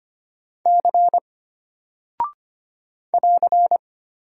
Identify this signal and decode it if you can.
Morse code, heard as a keyed tone, about 25 wpm, 700 Hz. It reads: C <AR>
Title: Single Character (Letter:Number:Punctuation:Prosign) Alt 25wpm